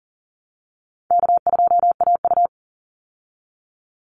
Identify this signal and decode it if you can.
Morse code, heard as a keyed tone, about 37 wpm, 700 Hz. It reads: K2AU
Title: Easy Call Signs Repeated 3x - 40wpm